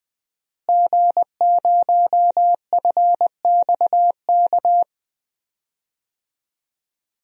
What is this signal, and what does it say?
Morse code, heard as a keyed tone, about 20 wpm, 700 Hz. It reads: G0FXK
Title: DXCC-100 Call Signs - 5 Characters - Repeated 2x - 20wpm